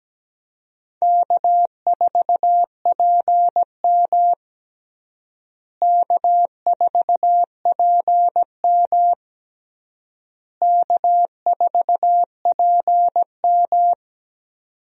Morse code, heard as a keyed tone, about 17 wpm, 700 Hz. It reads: K4PM K4PM K4PM
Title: USA Call Signs - 4 Characters - Repeated 3x - 17wpm